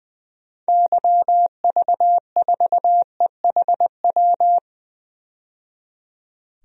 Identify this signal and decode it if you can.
Morse code, heard as a keyed tone, about 20 wpm, 700 Hz. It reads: YV4EHW
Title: DXCC-100 Call Signs - 6 Characters - 20wpm